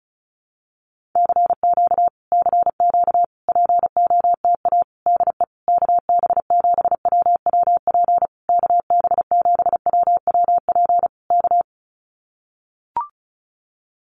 Morse code, heard as a keyed tone, about 34 wpm, 700 Hz. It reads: CQ CQ POTA DE KB7WWP KB7WWP K
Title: Common QSO Phrases Encore 35wpm